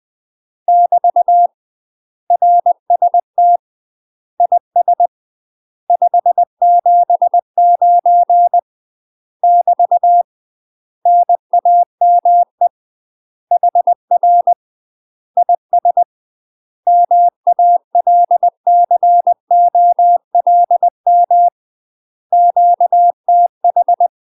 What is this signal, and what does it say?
Morse code, heard as a keyed tone, about 20 wpm, 700 Hz. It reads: <BT> RST IS 579 <BT> NAME HR IS MALCOLM QTH
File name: UK MASTER QSOs TRAINING / 20WPM / UK MASTER QSOs 0048